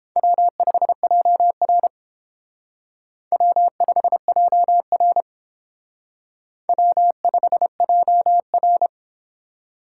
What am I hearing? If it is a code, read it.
Morse code, heard as a keyed tone, about 31 wpm, 700 Hz. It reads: W5JR W5JR W5JR